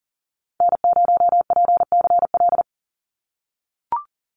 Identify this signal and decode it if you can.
Morse code, heard as a keyed tone, about 40 wpm, 700 Hz. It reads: N0PCL
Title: NA SOTA Activator Callsigns 40wpm